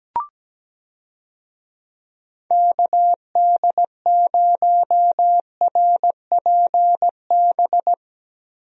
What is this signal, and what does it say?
Morse code, heard as a keyed tone, about 17 wpm, 700 Hz. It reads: KD0RPB
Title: USA Call Signs - 6 Characters - Repeated 3x - 17wpm